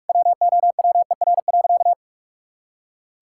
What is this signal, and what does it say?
Morse code, heard as a keyed tone, about 47 wpm, 700 Hz. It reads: WOWER.
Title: Pangram Challenge v2 - 45wpm